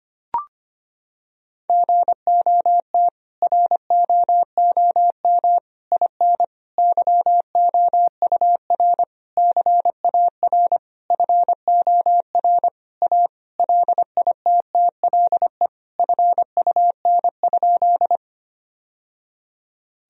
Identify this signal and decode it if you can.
Morse code, heard as a keyed tone, about 25 wpm, 700 Hz. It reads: GOT ROOM IN YOUR CAR FOR A LITTLE FUN?
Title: Sentences from Top 1000 Words - Review Part 1 - 25wpm